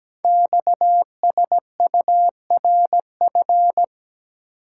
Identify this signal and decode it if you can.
Morse code, heard as a keyed tone, about 17 wpm, 700 Hz. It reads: XSURF
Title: Code Groups - 5 Letters - 17wpm